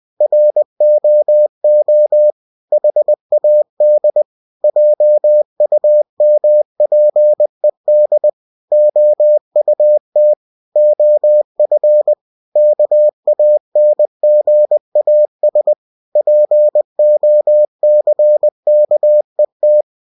Morse code, heard as a keyed tone, about 20 wpm, 600 Hz. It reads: ROO HAD JUMPED OUT OF KANGAS POCKET
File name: cw-books / winnie-the-pooh / winnie-20wpm-page-0063